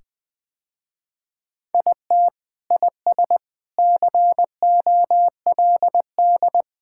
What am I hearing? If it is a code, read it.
Morse code, heard as a keyed tone, about 21 wpm, 700 Hz. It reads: IT IS COLD
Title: Sentences with 3-Word Phrases - Spoken Beginning - 20wpm